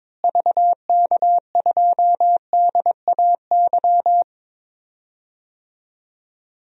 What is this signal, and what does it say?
Morse code, heard as a keyed tone, about 22 wpm, 700 Hz. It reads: VK2DAY
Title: DXCC-100 Call Signs - 6 Characters - Repeated 3x - 22wpm